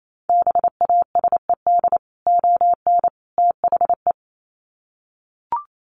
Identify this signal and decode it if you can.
Morse code, heard as a keyed tone, about 28 wpm, 700 Hz. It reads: BASED ON THE
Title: Sets of 3 Words 28wpm